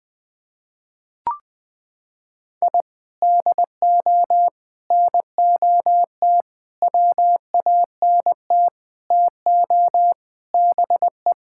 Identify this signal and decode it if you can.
Morse code, heard as a keyed tone, about 20 wpm, 700 Hz. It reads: I DO NOT WANT TO BE
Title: Sentences with 4-Word Phrases - 20wpm